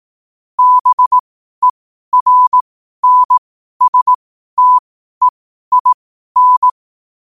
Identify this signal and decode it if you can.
Morse code, heard as a keyed tone, about 18 wpm, spaced as for 12 wpm, 1000 Hz. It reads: BERNSTEIN
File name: qotdfiles / 12wpm